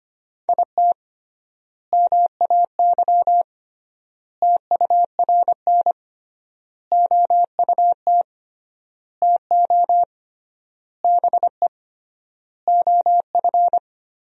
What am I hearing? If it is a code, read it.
Morse code, heard as a keyed tone, about 25 wpm, 700 Hz. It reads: IT MAY TURN OUT TO BE OF
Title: 3x Word-Spacing - Sentences from Top 400 Words - 25wpm